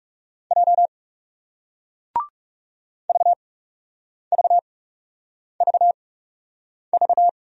Morse code, heard as a keyed tone, about 38 wpm, spaced as for 8 wpm, 700 Hz. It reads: J VVVV